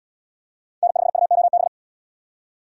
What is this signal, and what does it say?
Morse code, heard as a keyed tone, about 68 wpm, 700 Hz. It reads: N5UOB